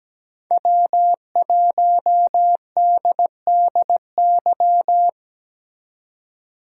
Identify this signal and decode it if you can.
Morse code, heard as a keyed tone, about 21 wpm, 700 Hz. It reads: W1DDY
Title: Speed Racing - USA 1x3 Call Signs 17wpm